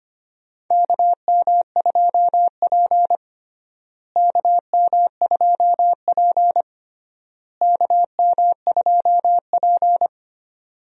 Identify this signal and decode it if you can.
Morse code, heard as a keyed tone, about 25 wpm, 700 Hz. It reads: KM2P KM2P KM2P
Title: USA Call Signs - 4 Characters - Repeated 3x - 25wpm